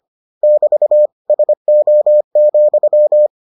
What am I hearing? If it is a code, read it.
Morse code, heard as a keyed tone, about 25 wpm, 600 Hz. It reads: <BT> SO,